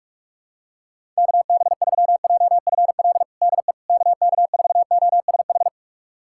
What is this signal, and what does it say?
Morse code, heard as a keyed tone, about 46 wpm, 700 Hz. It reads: KB3JFL DE KK4OSH
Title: USA Callsign DE Callsign 45wpm